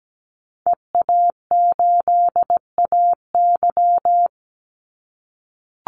Morse code, heard as a keyed tone, about 17 wpm, 700 Hz. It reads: EA8AY
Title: Call Signs - 17wpm